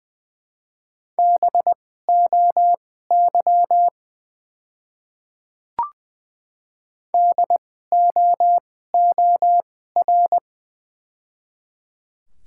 Morse code, heard as a keyed tone, about 20 wpm, spaced as for 14 wpm, 700 Hz. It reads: BOY DOOR
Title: 2x Character Spacing - Top 300 Words - 20wpm